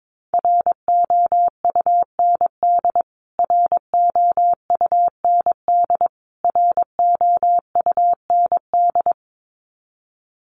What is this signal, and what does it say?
Morse code, heard as a keyed tone, about 22 wpm, 700 Hz. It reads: ROUND ROUND ROUND
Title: Top 500 Words Repeated 22wpm